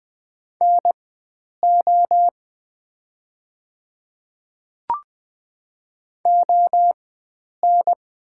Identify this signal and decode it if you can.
Morse code, heard as a keyed tone, about 20 wpm, spaced as for 9 wpm, 700 Hz. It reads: NO ON
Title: 4x Character Spacing - Top 100 Words - 20wpm